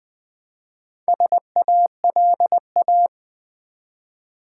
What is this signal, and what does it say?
Morse code, heard as a keyed tone, about 20 wpm, 700 Hz. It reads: SALA
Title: Build A Word - 1 Letter At A Time - 3 to 5-Letter Words - 20wpm